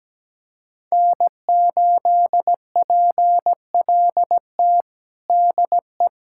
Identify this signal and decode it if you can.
Morse code, heard as a keyed tone, about 17 wpm, 700 Hz. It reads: N8PLT DE
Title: USA Callsign DE Callsign - 5 Characters - 17wpm